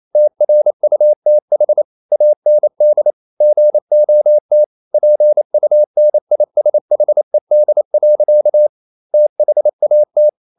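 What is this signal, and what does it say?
Morse code, heard as a keyed tone, about 28 wpm, 600 Hz. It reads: TRUTH AND GOT PUNISHED. THAT
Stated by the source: Morse code practice at 28 wpm, text of Ken Follett's novel Never in English, from CW med Gnister.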